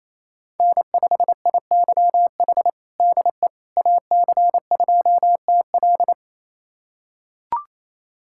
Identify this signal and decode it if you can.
Morse code, heard as a keyed tone, about 28 wpm, 700 Hz. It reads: N5IYH DE AC2TL
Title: USA Callsign DE Callsign - 5 Characters - 28wpm